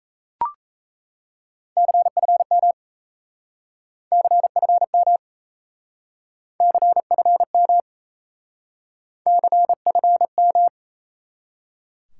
Morse code, heard as a keyed tone, about 37 wpm, 700 Hz. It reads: CFM CFM CFM CFM